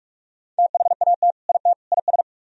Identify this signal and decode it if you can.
Morse code, heard as a keyed tone, about 45 wpm, 700 Hz. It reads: THAT IT IS